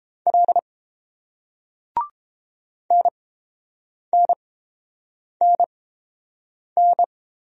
Morse code, heard as a keyed tone, about 30 wpm, spaced as for 7 wpm, 690 Hz. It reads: L NNNN